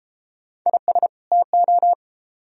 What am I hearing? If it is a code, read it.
Morse code, heard as a keyed tone, about 33 wpm, 700 Hz. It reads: IS TO